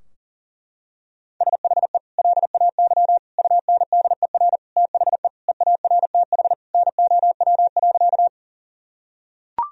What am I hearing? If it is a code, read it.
Morse code, heard as a keyed tone, about 40 wpm, 700 Hz. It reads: SHE LAY UNDER THE EARTH NOW.
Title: Sentences from Top 1000 Words - Review Part 2 - 40wpm